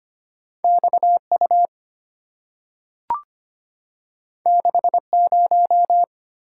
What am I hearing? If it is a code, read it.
Morse code, heard as a keyed tone, about 25 wpm, 700 Hz. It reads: X U 6 0
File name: Two Letter-Number Combinations - 25wpm